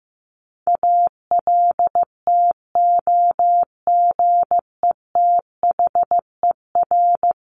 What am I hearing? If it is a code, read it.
Morse code, heard as a keyed tone, about 14 wpm, 700 Hz. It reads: ALTOGETHER
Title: Top 1501-1750 Words 15wpm